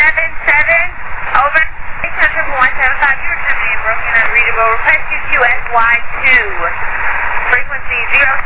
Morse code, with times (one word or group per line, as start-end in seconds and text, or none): none